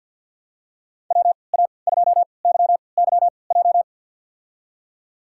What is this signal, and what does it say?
Morse code, heard as a keyed tone, about 48 wpm, spaced as for 28 wpm, 700 Hz. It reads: WA2YYJ
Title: USA Call Signs - 3x Character Spacing - 50wpm